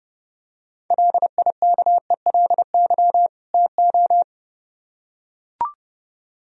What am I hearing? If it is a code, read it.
Morse code, heard as a keyed tone, about 30 wpm, 700 Hz. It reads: LIKELY TO